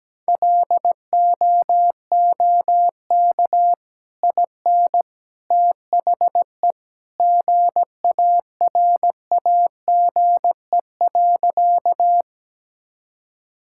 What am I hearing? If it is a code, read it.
Morse code, heard as a keyed tone, about 17 wpm, 700 Hz. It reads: LOOK IN THE GARAGE.
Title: Sentences from Top 1600 Words 17wpm